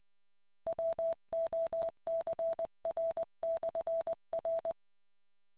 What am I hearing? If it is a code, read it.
Morse code, heard as a keyed tone, about 20 wpm, 660 Hz. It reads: WOCR/R